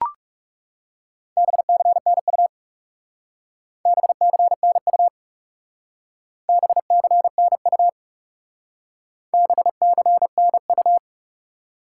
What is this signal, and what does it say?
Morse code, heard as a keyed tone, about 38 wpm, 700 Hz. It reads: BCNU BCNU BCNU BCNU